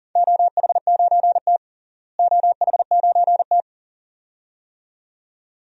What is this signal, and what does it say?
Morse code, heard as a keyed tone, about 40 wpm, 700 Hz. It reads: OH9T OH9T
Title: DXCC-100 Call Signs - 4 Characters - Repeated 2x - 40wpm